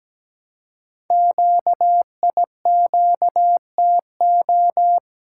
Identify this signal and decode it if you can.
Morse code, heard as a keyed tone, about 17 wpm, 700 Hz. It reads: QIQTO
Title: Code Groups - 5 Letters - 17wpm